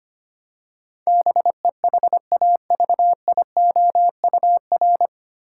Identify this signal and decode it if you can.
Morse code, heard as a keyed tone, about 25 wpm, 700 Hz. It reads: BEHAVIOUR
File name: practice / 25wpm / 9 Letter Words - First 3 Letters Given - 25wpm